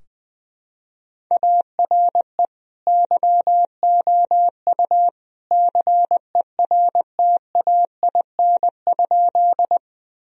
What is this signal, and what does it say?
Morse code, heard as a keyed tone, about 21 wpm, 700 Hz. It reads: ARE YOU CERTAIN?
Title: Sentences from Top 1000 Words - Review Part 2 - 20wpm